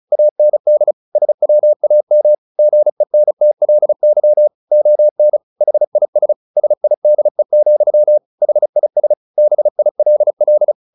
Morse code, heard as a keyed tone, about 34 wpm, 600 Hz. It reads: AND SWAM GENTLY ON HIS SIDE, HIS BILL
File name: the old man and the sea 2 35wpm